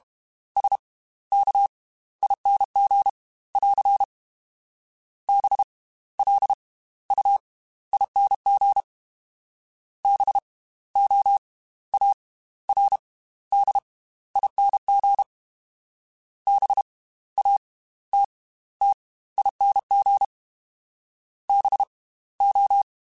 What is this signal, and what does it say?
Morse code, heard as a keyed tone, about 32 wpm, spaced as for 16 wpm, 800 Hz. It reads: SKING <AR> BLUING BOARDING BATTING BO
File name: Slow ing @30